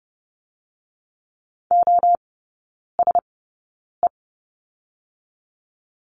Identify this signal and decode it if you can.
Morse code, heard as a keyed tone, about 30 wpm, spaced as for 9 wpm, 700 Hz. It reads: OSE
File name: 081 Lesson 14e - Number 2 - Three Characters 30wpm